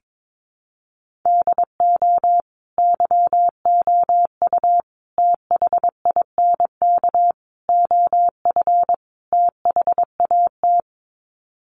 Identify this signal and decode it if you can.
Morse code, heard as a keyed tone, about 22 wpm, 700 Hz. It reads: DO YOU THINK OF THAT
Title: Sets of 5 Words Encore2 22wpm